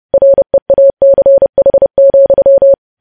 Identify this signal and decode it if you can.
Morse code, heard as a keyed tone, about 30 wpm, 550 Hz. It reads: REACH,
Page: news headlines in morse code at 30 wpm